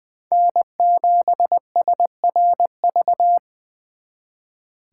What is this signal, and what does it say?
Morse code, heard as a keyed tone, about 24 wpm, 700 Hz. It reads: N7SRV
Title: Speed Racing - USA 1x3 Call Signs 20wpm